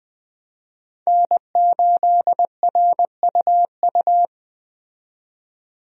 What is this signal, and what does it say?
Morse code, heard as a keyed tone, about 20 wpm, 700 Hz. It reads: N8RUU
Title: USA Call Signs - 5 Characters - 20wpm